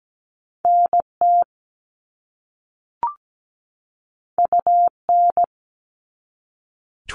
Morse code, heard as a keyed tone, about 17 wpm, 700 Hz. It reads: NT UN
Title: All Cut Numbers - 2-Digit 17wpm